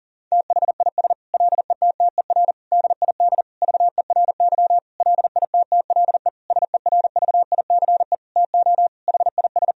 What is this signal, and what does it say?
Morse code, heard as a keyed tone, about 37 wpm, 700 Hz. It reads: THIS LETTER DID VERY LITTLE SERVICE TO HIS
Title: Sentences from Top 1000 Words - Review Part 4 - 40wpm